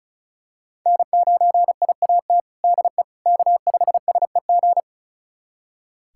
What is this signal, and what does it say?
Morse code, heard as a keyed tone, about 34 wpm, 700 Hz. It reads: N9IAT DE K5SEG
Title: USA Callsign DE Callsign - 5 Characters - 35wpm